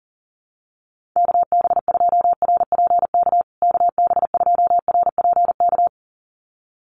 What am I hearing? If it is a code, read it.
Morse code, heard as a keyed tone, about 40 wpm, 700 Hz. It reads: KB2RPK KB2RPK
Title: USA Call Signs Repeated 2x 40wpm